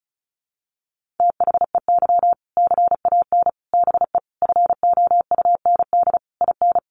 Morse code, heard as a keyed tone, about 34 wpm, 700 Hz. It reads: THEY CAN BE FOUND IN
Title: Sentences from Top 1500 Words 35wpm